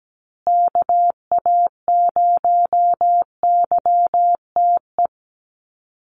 Morse code, heard as a keyed tone, about 17 wpm, 700 Hz. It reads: KA0YTE
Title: USA Call Signs Repeated 2x 17wpm